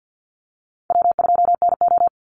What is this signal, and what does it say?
Morse code, heard as a keyed tone, about 48 wpm, 700 Hz. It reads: W2NO
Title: Easy Call Signs Repeated 3x - 50wpm